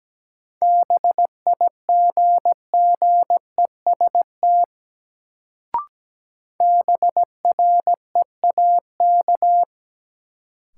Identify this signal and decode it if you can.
Morse code, heard as a keyed tone, about 17 wpm, 700 Hz. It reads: BIGGEST BREAK
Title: Sets of 4 Words - New Word Drill - 17wpm